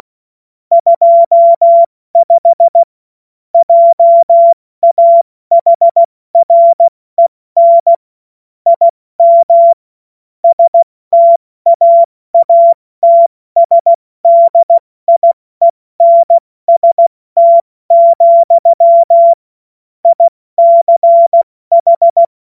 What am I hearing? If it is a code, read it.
Morse code, heard as a keyed tone, about 16 wpm, 680 Hz. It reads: 25 JAHREN IM STAATSDIENST, ICH
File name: Witz-16-September-2025-70-F80